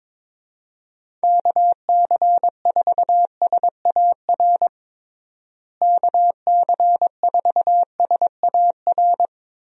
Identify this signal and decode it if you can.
Morse code, heard as a keyed tone, about 22 wpm, 700 Hz. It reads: KC4SAR KC4SAR
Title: USA Call Signs - 6 Characters - Repeated 2x - 22wpm